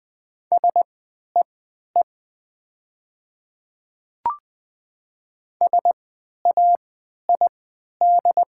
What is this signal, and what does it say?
Morse code, heard as a keyed tone, about 20 wpm, spaced as for 11 wpm, 700 Hz. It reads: SEE SAID
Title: 3x Character Spacing - Top 100 Words - 20wpm